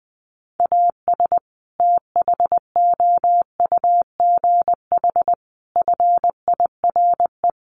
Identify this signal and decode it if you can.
Morse code, heard as a keyed tone, about 20 wpm, 700 Hz. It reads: AS THOUGH FIRE
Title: Sentences from Top 1500 Words 20wpm